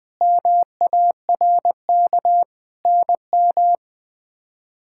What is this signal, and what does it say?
Morse code, heard as a keyed tone, about 21 wpm, 700 Hz. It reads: MARK NM
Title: K1USN SST QSO Practice - 20wpm